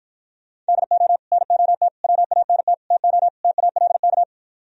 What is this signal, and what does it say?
Morse code, heard as a keyed tone, about 54 wpm, 700 Hz. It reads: DO NOT WANT TO TALK